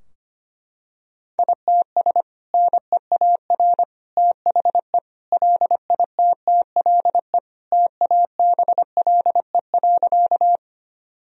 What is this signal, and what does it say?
Morse code, heard as a keyed tone, about 25 wpm, 700 Hz. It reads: ITS NEAR THE LITTLE TABLE.
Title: Sentences from Top 1000 Words - Review Part 2 - 25wpm